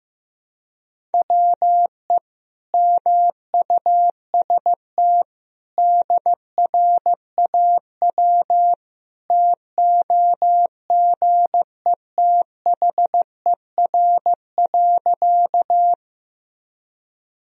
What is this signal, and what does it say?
Morse code, heard as a keyed tone, about 15 wpm, 700 Hz. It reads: WE MUST DRAW TOGETHER.
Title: Sentences from Top 1000 Words - Review Part 1 - 15wpm